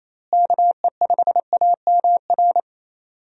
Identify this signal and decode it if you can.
Morse code, heard as a keyed tone, about 28 wpm, 700 Hz. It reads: KE5AMR
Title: USA Call Signs - 6 Characters - Repeated 3x - 28wpm